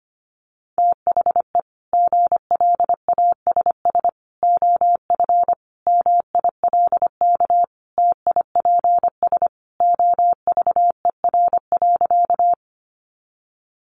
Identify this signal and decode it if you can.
Morse code, heard as a keyed tone, about 25 wpm, 700 Hz. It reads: THE GLASS OF MILK TIPS OVER.
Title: Additional Easy Phrases 25wpm